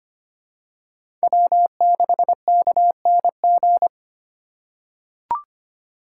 Morse code, heard as a keyed tone, about 25 wpm, 700 Hz. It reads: W6KNG
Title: USA Call Signs - 5 Characters - 2x Character Spacing - 25wpm